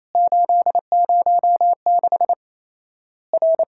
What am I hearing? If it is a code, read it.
Morse code, heard as a keyed tone, about 28 wpm, 700 Hz. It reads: 806 R
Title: Worked All Europe DX Contest - QTC Exchange - 28wpm